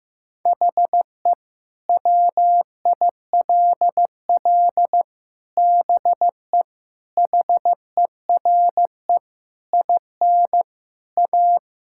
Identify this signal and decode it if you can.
Morse code, heard as a keyed tone, about 15 wpm, 700 Hz. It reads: HE WILL BE HERE IN A
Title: Sentences from Top 1000 Words - Review Part 5 - 15wpm